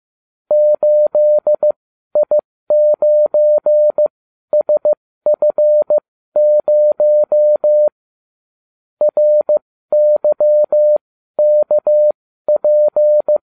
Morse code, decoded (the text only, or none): 8I9SF0 RYKP